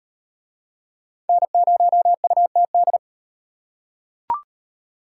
Morse code, heard as a keyed tone, about 35 wpm, 700 Hz. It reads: N0UTD